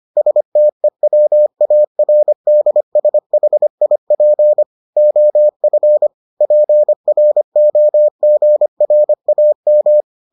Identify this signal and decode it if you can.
Morse code, heard as a keyed tone, about 25 wpm, 600 Hz. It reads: STEWARDSHIP OF PROGRAM